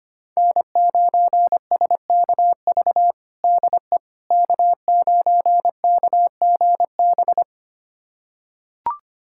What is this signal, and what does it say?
Morse code, heard as a keyed tone, about 25 wpm, 700 Hz. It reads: N9SKV DE K9KGB
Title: USA Callsign DE Callsign - 5 Characters - 25wpm